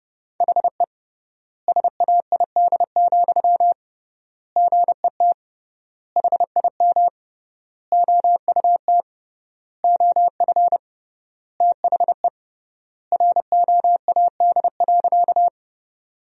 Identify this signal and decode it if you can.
Morse code, heard as a keyed tone, about 30 wpm, 700 Hz. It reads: HE SAID, GET HIM OUT OF THE ROAD.
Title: 3x Word-Spacing - Sentences from Top 400 Words - 30wpm